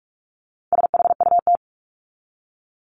Morse code, heard as a keyed tone, about 44 wpm, 700 Hz. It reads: SHUT
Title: Top 1751-2000 Words 45wpm